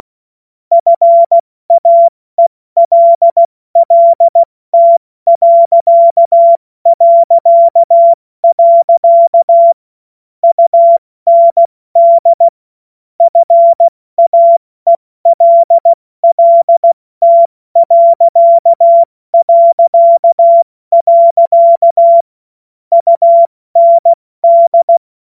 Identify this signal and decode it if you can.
Morse code, heard as a keyed tone, about 16 wpm, 680 Hz. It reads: FAELLT... UND FAELLT... UND
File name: Morsekurs / Witze / Witz 07. Dezember 2021, 70 F80